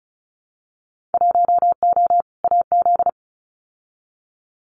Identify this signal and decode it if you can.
Morse code, heard as a keyed tone, about 34 wpm, 700 Hz. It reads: 1O AZ
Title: Winter Field Day Exchanges 35wpm